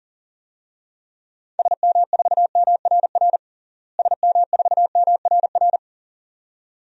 Morse code, heard as a keyed tone, about 40 wpm, 700 Hz. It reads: SM4MRR SM4MRR
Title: DXCC-100 Call Signs - 6 Characters - Repeated 2x - 40wpm